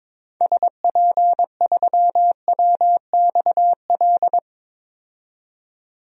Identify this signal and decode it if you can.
Morse code, heard as a keyed tone, about 22 wpm, 700 Hz. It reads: SP3WXL
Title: DXCC-100 Call Signs - 6 Characters - 22wpm